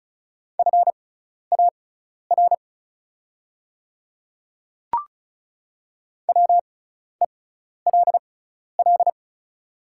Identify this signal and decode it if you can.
Morse code, heard as a keyed tone, about 36 wpm, spaced as for 12 wpm, 690 Hz. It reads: FAR WELL
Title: 6x Character Spacing - Top 300 Words - 35wpm